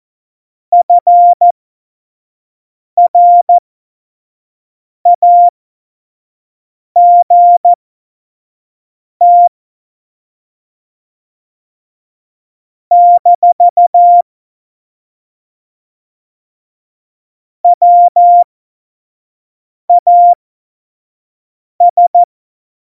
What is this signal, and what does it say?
Morse code, heard as a keyed tone, about 14 wpm, spaced as for 5 wpm, 700 Hz. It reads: FRAGT - WAS
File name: Morsekurs / Witze / Witz 02. April 2019, 25 F70